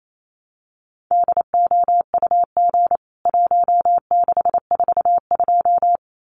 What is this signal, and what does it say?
Morse code, heard as a keyed tone, about 28 wpm, 700 Hz. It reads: DOUG 1642
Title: CWT Exchange 28wpm